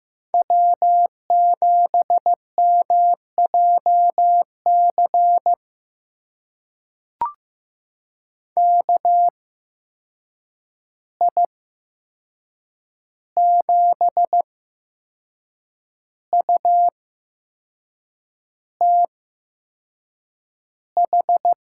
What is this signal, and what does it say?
Morse code, heard as a keyed tone, about 15 wpm, spaced as for 6 wpm, 700 Hz. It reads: W7MJC KI7UTH
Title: USA Call Signs - 8x Character Spacing - 15wpm